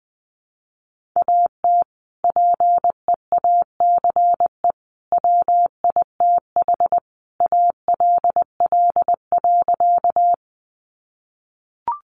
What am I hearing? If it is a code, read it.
Morse code, heard as a keyed tone, about 20 wpm, 700 Hz. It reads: AT PEACE WITH ALL.
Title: Sentences from Top 1500 Words 20wpm